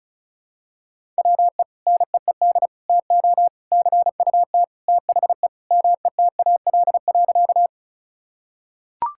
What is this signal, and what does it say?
Morse code, heard as a keyed tone, about 34 wpm, 700 Hz. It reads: WE NEED TO CUT THE METAL.
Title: Sentences from Top 1000 Words - Review Part 6 - 35wpm